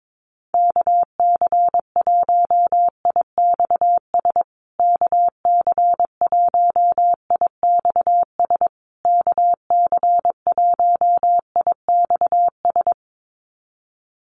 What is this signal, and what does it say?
Morse code, heard as a keyed tone, about 22 wpm, 700 Hz. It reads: KC1IXS KC1IXS KC1IXS
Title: USA Call Signs Repeated 3x 22wpm